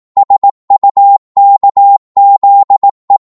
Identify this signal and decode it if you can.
Morse code, heard as a keyed tone, about 18 wpm, 800 Hz. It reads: SUKZE
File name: Morse / ltr-18wpm